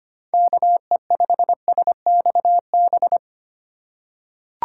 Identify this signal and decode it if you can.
Morse code, heard as a keyed tone, about 25 wpm, 700 Hz. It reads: KE5SXB
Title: USA Call Signs - 6 Characters - 2x Character Spacing - 25wpm